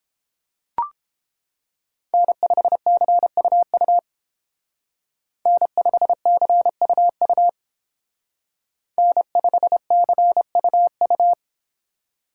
Morse code, heard as a keyed tone, about 31 wpm, 700 Hz. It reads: N5CUU N5CUU N5CUU